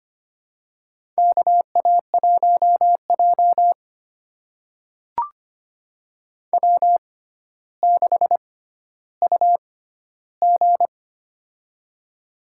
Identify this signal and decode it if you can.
Morse code, heard as a keyed tone, about 25 wpm, spaced as for 13 wpm, 700 Hz. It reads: KA1J W6UG
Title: USA Call Signs - 4 Characters - 6x Character Spacing - 25wpm